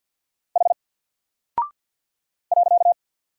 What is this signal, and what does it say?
Morse code, heard as a keyed tone, about 50 wpm, 700 Hz. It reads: H .
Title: Single Character (Letter:Number:Punctuation:Prosign) 50wpm